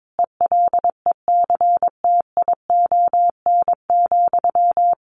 Morse code, heard as a keyed tone, about 22 wpm, 700 Hz. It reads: ELECTION,
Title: Sentences from Top 1500 Words 22wpm